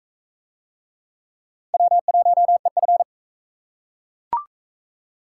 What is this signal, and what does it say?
Morse code, heard as a keyed tone, about 42 wpm, 700 Hz. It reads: W1EF